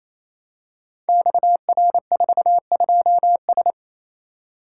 Morse code, heard as a keyed tone, about 28 wpm, 700 Hz. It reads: XR42S
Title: Code Groups - 5 Letter:Numbers - 28wpm